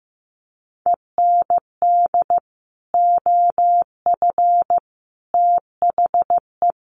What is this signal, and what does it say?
Morse code, heard as a keyed tone, about 15 wpm, 700 Hz. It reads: END OF THE
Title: Sets of 4 Words 15wpm